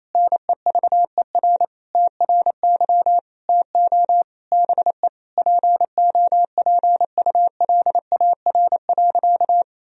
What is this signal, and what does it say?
Morse code, heard as a keyed tone, about 27 wpm, 700 Hz. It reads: NEVER TRY TO BE POPULAR.
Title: Sentences from Top 1000 Words - Review Part 3 - 28wpm